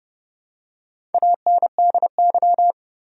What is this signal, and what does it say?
Morse code, heard as a keyed tone, about 30 wpm, 700 Hz. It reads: ANDY
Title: USA Common Names - Max 4 Letters - 30wpm